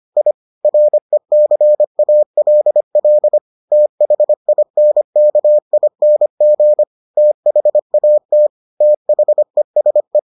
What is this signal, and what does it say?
Morse code, heard as a keyed tone, about 25 wpm, 600 Hz. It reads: I RECALL THINKING THAT THESE